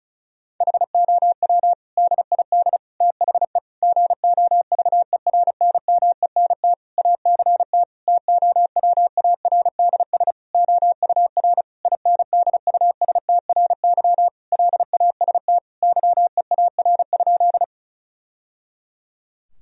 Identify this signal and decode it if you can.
Morse code, heard as a keyed tone, about 34 wpm, 700 Hz. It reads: HOW DID THE GOVERNMENT ACT TOWARDS OUR INDUSTRY LAST YEAR?
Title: Sentences from Top 1000 Words - Review Part 4 - 35wpm